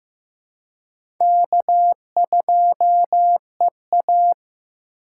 Morse code, heard as a keyed tone, about 15 wpm, 700 Hz. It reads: K2EA
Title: USA Call Signs - 4x Character Spacing - 15wpm